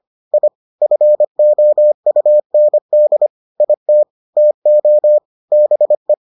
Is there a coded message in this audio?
Yes: Morse code from 0.3 to 6.1 s, heard as a keyed tone, about 25 wpm, 600 Hz. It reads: I FOUND IT TO BE